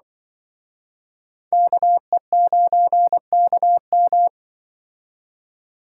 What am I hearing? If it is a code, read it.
Morse code, heard as a keyed tone, about 24 wpm, 700 Hz. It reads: KE9KM
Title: Speed Racing - USA 2x2 Call Signs 20wpm